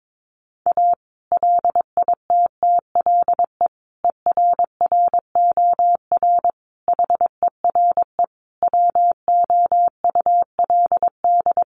Morse code, heard as a keyed tone, about 22 wpm, 700 Hz. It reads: A LITTLE ERROR HERE WOULD
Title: A Sound Of Thunder 22wpm